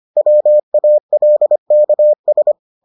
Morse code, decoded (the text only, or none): WALKS